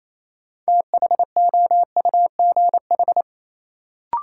Morse code, heard as a keyed tone, about 28 wpm, 700 Hz. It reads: THOUGH
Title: Sentences from Top 400 Words - New Word Drill - 28wpm